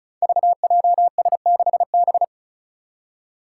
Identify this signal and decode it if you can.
Morse code, heard as a keyed tone, about 36 wpm, 700 Hz. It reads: VJS6B
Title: Five Challenging Letter-Number Combinations 35wpm